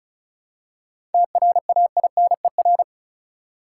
Morse code, heard as a keyed tone, about 34 wpm, 700 Hz. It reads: TRAINER
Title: Seven Letter Words - 35wpm